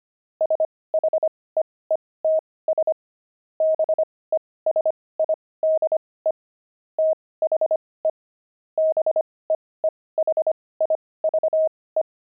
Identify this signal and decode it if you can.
Morse code, heard as a keyed tone, about 25 wpm, spaced as for 18 wpm, 630 Hz. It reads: SHEETS BESIDE THE BEEHIVE
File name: pangram-fs-18-cs-25-beehive